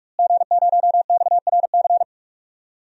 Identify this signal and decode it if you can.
Morse code, heard as a keyed tone, about 42 wpm, 700 Hz. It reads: G0XRC
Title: DXCC-100 Call Signs - 5 Characters - Repeated 2x - 45wpm